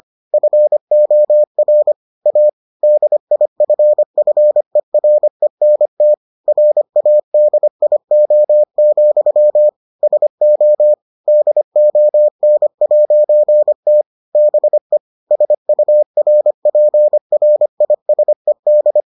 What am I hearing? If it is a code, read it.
Morse code, heard as a keyed tone, about 25 wpm, 600 Hz. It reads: FOR A DIFFERENT RADIO, SO DON'T BE SURPRISED